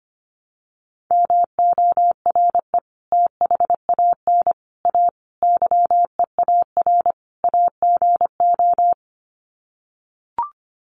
Morse code, heard as a keyed tone, about 25 wpm, 700 Hz. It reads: MORE THAN A YEAR AGO
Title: Sets of 5 Words Encore2 25wpm